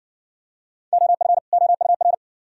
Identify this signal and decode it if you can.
Morse code, heard as a keyed tone, about 56 wpm, 700 Hz. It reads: OF OUR